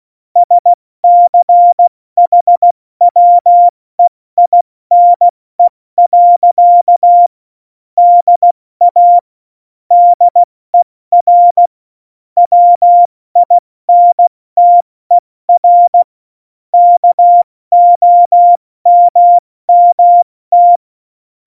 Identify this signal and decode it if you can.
Morse code, heard as a keyed tone, about 16 wpm, 700 Hz. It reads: SCHWEINE. DA DER WINTER KOMMT
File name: Morsekurs / Witze / Witz 21. Januar 2020, 70 F80